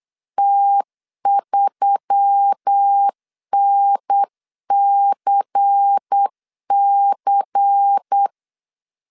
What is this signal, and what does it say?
Morse code, heard as a keyed tone, about 8 wpm, 800 Hz. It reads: T3NCC
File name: 8wpm